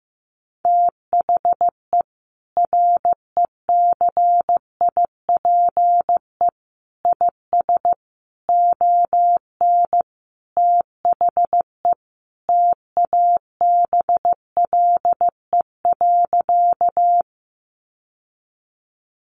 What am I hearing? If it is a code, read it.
Morse code, heard as a keyed tone, about 15 wpm, 700 Hz. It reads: THE RECIPE IS ON THE TABLE.
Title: Sentences from Top 1200 Words 15wpm